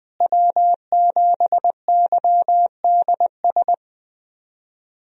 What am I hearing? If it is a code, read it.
Morse code, heard as a keyed tone, about 20 wpm, 700 Hz. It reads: W7YDS
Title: USA Call Signs - 5 Characters - Repeated 3x - 20wpm